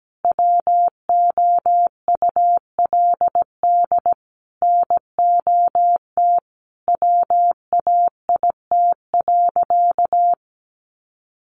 Morse code, heard as a keyed tone, about 17 wpm, 700 Hz. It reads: WOULD NOT WAIT.
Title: Sentences from Top 1400 Words 17wpm